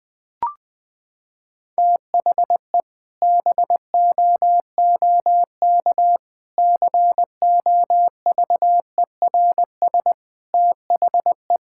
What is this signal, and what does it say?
Morse code, heard as a keyed tone, about 21 wpm, 700 Hz. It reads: THE BOOK COVERS THE
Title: Sentences with 4-Word Phrases - 20wpm